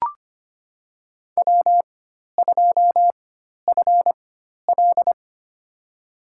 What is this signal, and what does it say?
Morse code, heard as a keyed tone, about 25 wpm, spaced as for 12 wpm, 700 Hz. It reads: W2FL
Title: USA Call Signs - 4 Characters - 4x Character Spacing - 25wpm